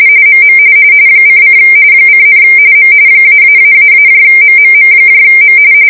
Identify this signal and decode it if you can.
RTTY, about 45.45 baud. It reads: MM :: :, M -----MMMM